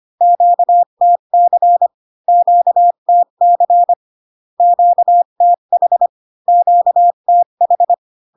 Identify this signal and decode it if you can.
Morse code, heard as a keyed tone, about 25 wpm, 700 Hz. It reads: QTC QTC QTH QTH